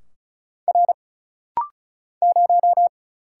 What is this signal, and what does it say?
Morse code, heard as a keyed tone, about 35 wpm, 700 Hz. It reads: R 0
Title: Single Letter-Number Alt - Rapid-Fire 35wpm